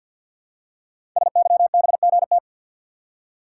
Morse code, heard as a keyed tone, about 50 wpm, 700 Hz. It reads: SYBGT
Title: Code Groups - 5 Letters - 50wpm